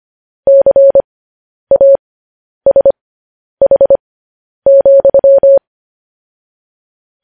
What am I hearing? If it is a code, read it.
Morse code, heard as a keyed tone, about 25 wpm, spaced as for 10 wpm, 550 Hz. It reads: CASH,
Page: news-headlines-in-morse-code-at-10-wpm